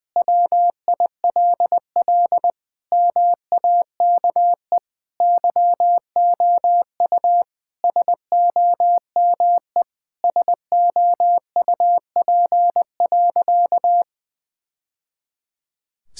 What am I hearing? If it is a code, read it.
Morse code, heard as a keyed tone, about 20 wpm, 700 Hz. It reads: WILL MAKE YOU SOME SOUP.
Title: Sentences from Top 1000 Words - Review Part 6 - 20wpm